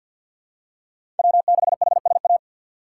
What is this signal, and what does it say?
Morse code, heard as a keyed tone, about 47 wpm, 700 Hz. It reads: W6HSA
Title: USA Call Signs - 5 Characters - 6x Character Spacing - 50wpm